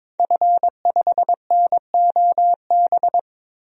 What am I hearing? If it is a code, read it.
Morse code, heard as a keyed tone, about 22 wpm, 700 Hz. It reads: F5NOB
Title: DXCC-100 Call Signs - 5 Characters - 22wpm